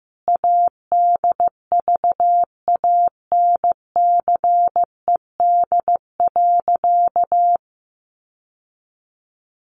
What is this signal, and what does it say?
Morse code, heard as a keyed tone, about 15 wpm, 700 Hz. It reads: ADVANCED.
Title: Sentences from Top 1200 Words 15wpm